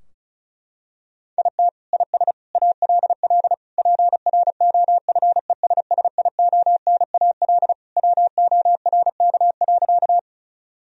Morse code, heard as a keyed tone, about 34 wpm, 700 Hz. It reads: IT IS ALL PROFESSIONAL WORK.
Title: Sentences from Top 1000 Words - Review Part 4 - 35wpm